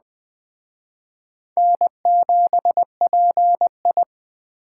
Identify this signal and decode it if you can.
Morse code, heard as a keyed tone, about 19 wpm, 700 Hz. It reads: N7PI